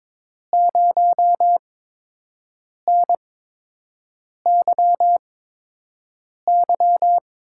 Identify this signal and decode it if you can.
Morse code, heard as a keyed tone, about 22 wpm, spaced as for 6 wpm, 700 Hz. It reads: 0NYY
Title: USA Call Signs - 5 Characters - 8x Character Spacing - 22wpm